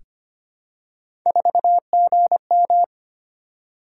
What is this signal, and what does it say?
Morse code, heard as a keyed tone, about 25 wpm, 700 Hz. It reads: 4GM
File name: Three Letter-Number Combinations - 25wpm